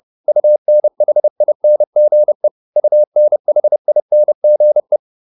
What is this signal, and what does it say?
Morse code, heard as a keyed tone, about 30 wpm, 600 Hz. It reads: UNHINGE UNHINGE